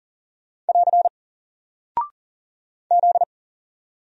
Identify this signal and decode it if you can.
Morse code, heard as a keyed tone, about 40 wpm, 700 Hz. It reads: <AR> Z
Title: Single Character (Letter:Number:Punctuation:Prosign) 40wpm